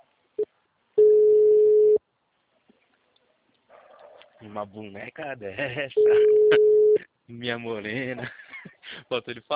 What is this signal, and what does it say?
Ringback tone (Europe, 425 Hz), with speech.